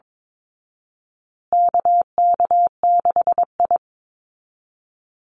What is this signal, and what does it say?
Morse code, heard as a keyed tone, about 22 wpm, 700 Hz. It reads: KK6I
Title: Call Signs Repeated 2x - 22wpm